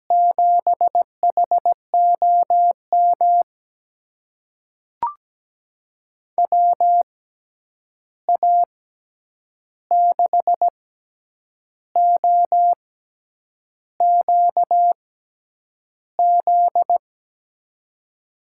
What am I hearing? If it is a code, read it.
Morse code, heard as a keyed tone, about 17 wpm, spaced as for 8 wpm, 700 Hz. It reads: 7HOM WA6OQZ
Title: USA Call Signs - 6x Character Spacing - 17wpm